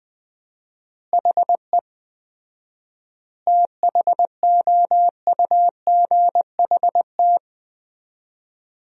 Morse code, heard as a keyed tone, about 20 wpm, 700 Hz. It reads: HE THOUGHT
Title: 4x Word-Spacing - Sentences from Top 300 Words - 20wpm